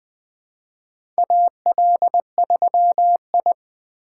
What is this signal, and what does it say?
Morse code, heard as a keyed tone, about 20 wpm, 700 Hz. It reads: AL3I
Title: USA Call Signs - 4 Characters - 4x Character Spacing - 20wpm